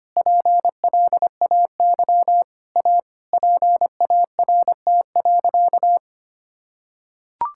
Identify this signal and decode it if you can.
Morse code, heard as a keyed tone, about 25 wpm, 700 Hz. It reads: PLAY A PART.
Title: Sentences from Top 1000 Words - Review Part 5 - 25wpm